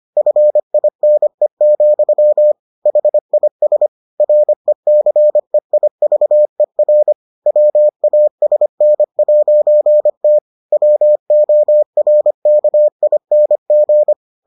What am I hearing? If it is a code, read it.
Morse code, heard as a keyed tone, about 25 wpm, 600 Hz. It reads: FINE, HIS RECEIVER WASN'T WORKING